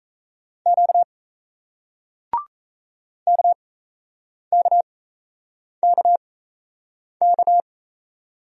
Morse code, heard as a keyed tone, about 38 wpm, spaced as for 8 wpm, 700 Hz. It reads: Q KKKK